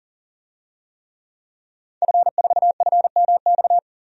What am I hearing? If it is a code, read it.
Morse code, heard as a keyed tone, about 40 wpm, 700 Hz. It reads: F4FMX
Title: DXCC-100 Call Signs - 5 Characters - 40wpm